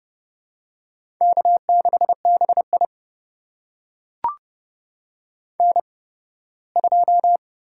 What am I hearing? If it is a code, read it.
Morse code, heard as a keyed tone, about 29 wpm, 700 Hz. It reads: K 6 B I N 2
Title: USA Call Signs - 4 Characters - 8x Character Spacing - 30wpm